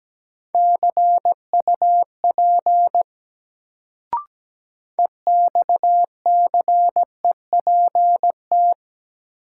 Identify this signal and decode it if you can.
Morse code, heard as a keyed tone, about 17 wpm, 700 Hz. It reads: CUP EXCEPT
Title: Sentences from Top 1600 Words - New Word Drill Alt - 17wpm